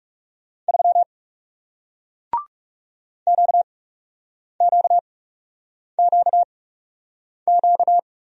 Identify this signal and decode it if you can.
Morse code, heard as a keyed tone, about 37 wpm, spaced as for 8 wpm, 700 Hz. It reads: 3 QQQQ
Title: Speed-Racing Single Letter-Number 30wpm